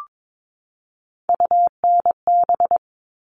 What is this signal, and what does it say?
Morse code, heard as a keyed tone, about 22 wpm, 700 Hz. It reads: UNB
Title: All Cut Numbers - 3-Digit 22wpm